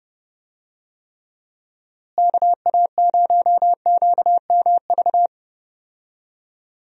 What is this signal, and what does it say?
Morse code, heard as a keyed tone, about 32 wpm, 700 Hz. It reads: KA0QMV
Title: USA Call Signs - 6 Characters - Repeated 3x - 30wpm